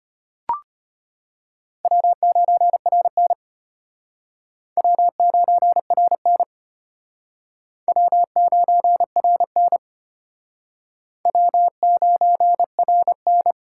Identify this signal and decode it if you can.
Morse code, heard as a keyed tone, about 32 wpm, 700 Hz. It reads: W9RN W9RN W9RN W9RN